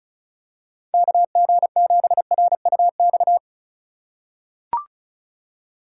Morse code, heard as a keyed tone, about 36 wpm, 700 Hz. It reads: KG7RUX
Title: USA Call Signs - 6 Characters - 3x Character Spacing - 35wpm